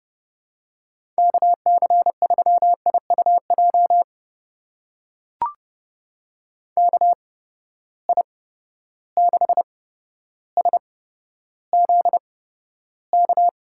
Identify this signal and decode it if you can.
Morse code, heard as a keyed tone, about 30 wpm, spaced as for 13 wpm, 700 Hz. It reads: KC3IUJ KI6SZK
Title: USA Call Signs - 6 Characters - 8x Character Spacing - 30wpm